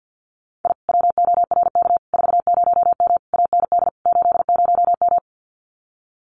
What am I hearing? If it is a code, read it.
Morse code, heard as a keyed tone, about 50 wpm, 700 Hz. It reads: I WORK 40M AND 80M
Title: Common QSO Phrases Encore 50wpm